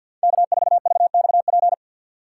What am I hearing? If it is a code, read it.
Morse code, heard as a keyed tone, about 46 wpm, 700 Hz. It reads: K4VXP
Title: USA Call Signs - 5 Characters - 2x Character Spacing - 50wpm